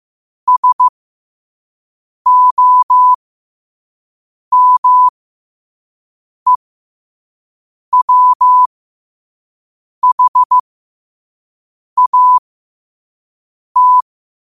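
Morse code, heard as a keyed tone, about 15 wpm, spaced as for 5 wpm, 1000 Hz. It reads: SOMEWHAT
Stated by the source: Morse code quote of the day at 5 wpm.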